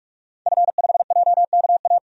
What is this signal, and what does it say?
Morse code, heard as a keyed tone, about 46 wpm, 700 Hz. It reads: F5JKA
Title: DXCC-100 Call Signs - 5 Characters - 45wpm